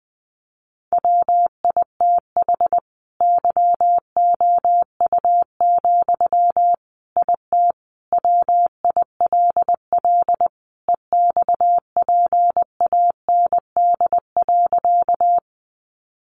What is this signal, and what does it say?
Morse code, heard as a keyed tone, about 20 wpm, 700 Hz. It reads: WITH YOU, IT WILL EXPAND.
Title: Sentences from Top 1400 Words 20wpm